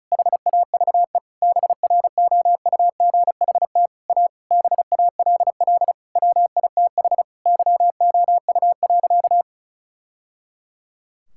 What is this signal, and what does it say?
Morse code, heard as a keyed tone, about 34 wpm, 700 Hz. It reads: HAVE BROUGHT A BALL WITH YOU.
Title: Sentences from Top 1000 Words - Review Part 6 - 35wpm